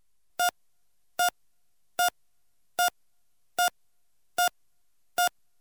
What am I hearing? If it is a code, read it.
There is no code.